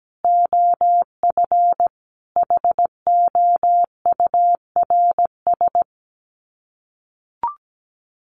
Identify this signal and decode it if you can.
Morse code, heard as a keyed tone, about 17 wpm, 700 Hz. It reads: OF HOURS
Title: Sets of 5 Words Encore2 17wpm